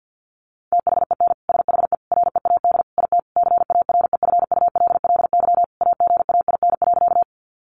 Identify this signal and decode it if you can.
Morse code, heard as a keyed tone, about 48 wpm, 700 Hz. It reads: THEN SHE READ IT CAREFULLY AGAIN.
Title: Sentences from Top 1300 Words 50wpm